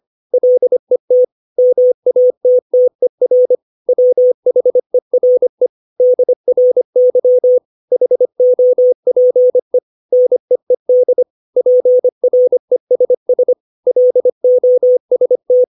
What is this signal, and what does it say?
Morse code, heard as a keyed tone, about 25 wpm, 500 Hz. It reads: LET MATTER WHERE DRY HOPE NEED PRESS LOST
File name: 632 words / 632 Words 25WPM